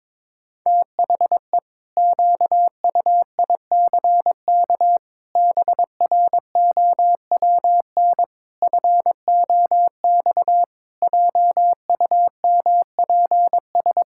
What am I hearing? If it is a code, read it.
Morse code, heard as a keyed tone, about 21 wpm, 700 Hz. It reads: THE QUICK BROWN FOX JUMPS
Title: Pangram Challenge v2 - 22wpm